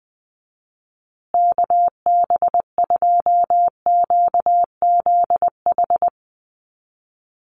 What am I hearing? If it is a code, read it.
Morse code, heard as a keyed tone, about 20 wpm, 700 Hz. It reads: KB2QZH
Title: USA Call Signs Repeated 2x 20wpm